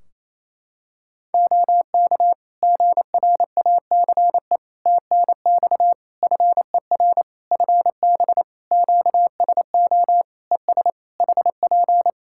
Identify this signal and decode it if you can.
Morse code, heard as a keyed tone, about 28 wpm, 700 Hz. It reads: OK GRACE TNX FER FB QSO ES HP
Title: Practice QSO - Exchange 4 - Ending - 28wpm